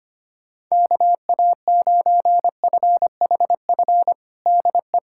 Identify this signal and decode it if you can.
Morse code, heard as a keyed tone, about 25 wpm, 700 Hz. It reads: KA9FHF DE
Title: USA Callsign DE Callsign - 6 Characters - 25wpm